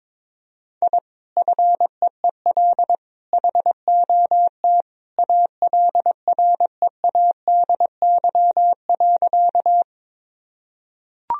Morse code, heard as a keyed tone, about 22 wpm, 690 Hz. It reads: I FEEL HOT ALREADY.
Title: Sentences from Top 1000 Words - Review Part 3 - 22wpm